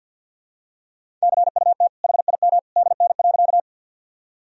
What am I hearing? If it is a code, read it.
Morse code, heard as a keyed tone, about 52 wpm, 700 Hz. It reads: CUT HIM DN.
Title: Sentences from Top 300 Words and Abbreviations 50wpm